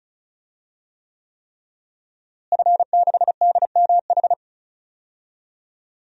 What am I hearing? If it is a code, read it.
Morse code, heard as a keyed tone, about 36 wpm, 700 Hz. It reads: F6DMH
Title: DXCC-100 Call Signs - 5 Characters - Repeated 3x - 35wpm